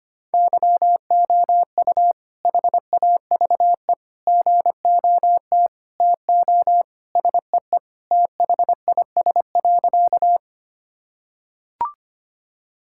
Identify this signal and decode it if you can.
Morse code, heard as a keyed tone, about 25 wpm, 700 Hz. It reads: YOU HAVE GOT TO SEE THIS.
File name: practice / 25wpm / Sentences with 2-Word Phrases - 25wpm